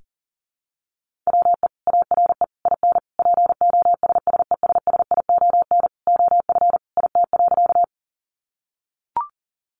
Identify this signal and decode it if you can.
Morse code, heard as a keyed tone, about 37 wpm, 700 Hz. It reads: WE ARE IN POSSESSION OF IT.
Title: Sentences from Top 1400 Words 40wpm